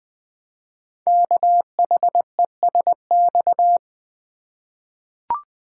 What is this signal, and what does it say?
Morse code, heard as a keyed tone, about 20 wpm, 700 Hz. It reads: KHESX
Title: Code Groups - 5 Letters - 20wpm